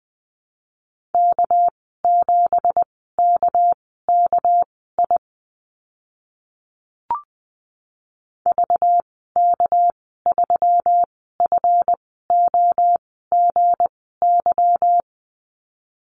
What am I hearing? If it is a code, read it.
Morse code, heard as a keyed tone, about 20 wpm, spaced as for 14 wpm, 700 Hz. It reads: K7KKI VK3FOGY
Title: 213 Lesson 34g - Number 8 - Call Signs 20wpm